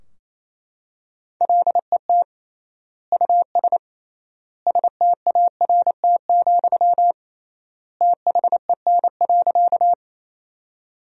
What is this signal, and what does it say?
Morse code, heard as a keyed tone, about 27 wpm, 700 Hz. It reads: LET US START, THEN.
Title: 3x Word-Spacing - Sentences from Top 300 Words - 28wpm